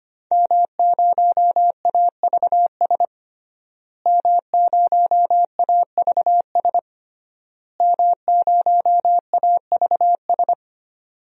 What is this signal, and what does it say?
Morse code, heard as a keyed tone, about 25 wpm, 700 Hz. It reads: M0AVS M0AVS M0AVS
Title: DXCC-100 Call Signs - 5 Characters - Repeated 3x - 25wpm